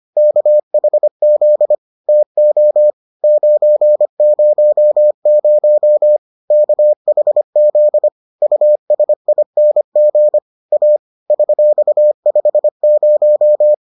Morse code, heard as a keyed tone, about 25 wpm, 600 Hz. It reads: KHZ TO 900 KHZ USING A $50